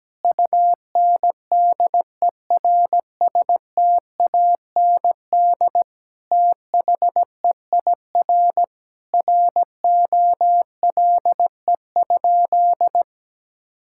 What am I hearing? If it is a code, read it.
Morse code, heard as a keyed tone, about 17 wpm, 700 Hz. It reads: UNDERSTAND THEIR ROLE?
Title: Sentences from Top 1000 Words - Review Part 6 - 17wpm